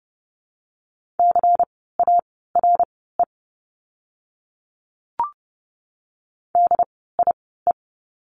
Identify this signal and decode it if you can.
Morse code, heard as a keyed tone, about 30 wpm, spaced as for 17 wpm, 700 Hz. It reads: CARE DIE
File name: Lesson 16f - Letter c - Words 30wpm